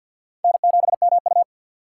Morse code, heard as a keyed tone, about 51 wpm, 700 Hz. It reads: N7MU
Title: USA Call Signs - 3x Character Spacing - 50wpm